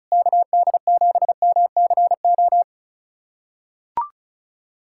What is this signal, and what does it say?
Morse code, heard as a keyed tone, about 36 wpm, 700 Hz. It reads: KD7MCO
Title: USA Callsign DE Callsign 35wpm